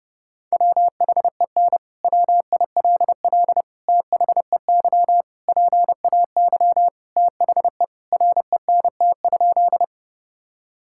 Morse code, heard as a keyed tone, about 30 wpm, 700 Hz. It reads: WHEN WILL THEY PAY THE RENT?
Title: Sentences from Top 1000 Words - Review Part 5 - 30wpm